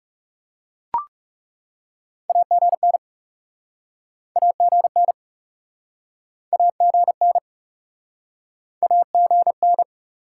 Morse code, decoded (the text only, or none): AGN AGN AGN AGN